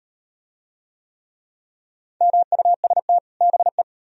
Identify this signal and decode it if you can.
Morse code, heard as a keyed tone, about 38 wpm, 700 Hz. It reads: MUST BE